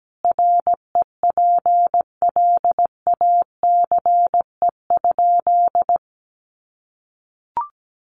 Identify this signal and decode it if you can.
Morse code, heard as a keyed tone, about 17 wpm, 700 Hz. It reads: REPLACE?
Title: Sentences from Top 1400 Words 17wpm